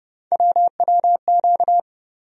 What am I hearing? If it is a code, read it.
Morse code, heard as a keyed tone, about 30 wpm, 700 Hz. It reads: WWQ
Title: USA Call Sign Postfixes - 3 Letters 30wpm